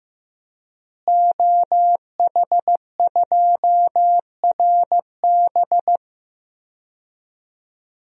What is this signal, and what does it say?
Morse code, heard as a keyed tone, about 15 wpm, 700 Hz. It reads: OH2RB
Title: DXCC-100 Call Signs - 5 Characters - Repeated 2x - 15wpm